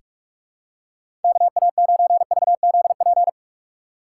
Morse code, heard as a keyed tone, about 42 wpm, 700 Hz. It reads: KA9VZP
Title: USA Call Signs - 6 Characters - 4x Character Spacing - 45wpm